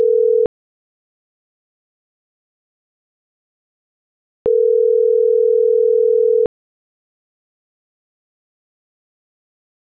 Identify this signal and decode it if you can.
Ringback tone (North America), 440 + 480 Hz.